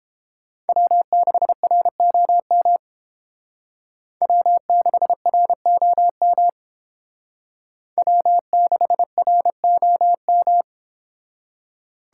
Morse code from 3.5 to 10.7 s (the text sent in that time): W6ROM W6ROM